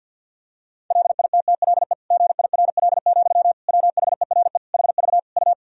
Morse code, heard as a keyed tone, about 50 wpm, 700 Hz. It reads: LITTLE GIRL, WHERE HV U
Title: Sentences from Top 400 Words and Abbreviations 50wpm